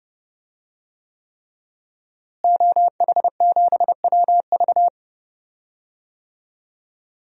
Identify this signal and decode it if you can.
Morse code, heard as a keyed tone, about 30 wpm, 700 Hz. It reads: OH7WV
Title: DXCC-100 Call Signs - 5 Characters - Repeated 3x - 30wpm